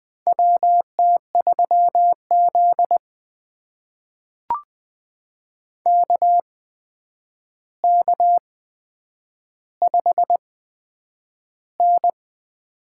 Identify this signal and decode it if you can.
Morse code, heard as a keyed tone, about 20 wpm, spaced as for 9 wpm, 700 Hz. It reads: WT3Z KK5N
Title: USA Call Signs - 4 Characters - 8x Character Spacing - 20wpm